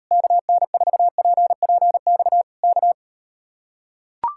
Morse code, heard as a keyed tone, about 38 wpm, 700 Hz. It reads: KN4PPX K